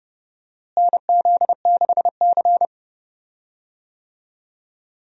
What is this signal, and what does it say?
Morse code, heard as a keyed tone, about 30 wpm, 700 Hz. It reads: NZ6C
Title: USA Call Signs - 4 Characters - Repeated 2x - 30wpm